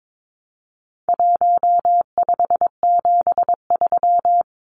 Morse code, heard as a keyed tone, about 22 wpm, 700 Hz. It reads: 1573
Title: Numbers - Contest Style - 4-Digit 22wpm